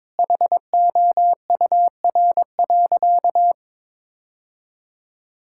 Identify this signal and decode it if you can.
Morse code, heard as a keyed tone, about 22 wpm, 700 Hz. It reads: HOUR.
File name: practice / 22wpm / Sentences with 3-Word Phrases - 22wpm